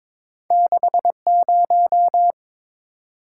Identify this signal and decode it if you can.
Morse code, heard as a keyed tone, about 21 wpm, 700 Hz. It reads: 60
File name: practice / 22wpm / Two Letter-Number Combinations - 22wpm